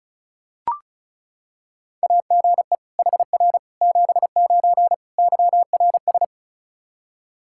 Morse code, heard as a keyed tone, about 36 wpm, 700 Hz. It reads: AGE HR 79 YRS
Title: Practice QSO - Exchange 3 - More Info - 35wpm